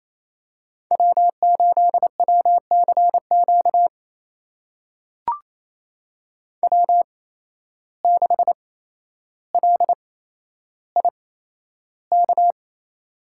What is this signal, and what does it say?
Morse code, heard as a keyed tone, about 27 wpm, spaced as for 12 wpm, 700 Hz. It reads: W8WCQ W6LIK
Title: USA Call Signs - 5 Characters - 8x Character Spacing - 28wpm